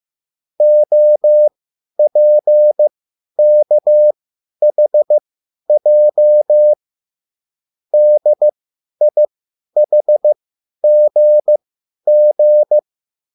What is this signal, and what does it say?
Morse code, heard as a keyed tone, about 15 wpm, spaced as for 10 wpm, 600 Hz. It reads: OPKHJ DIHGG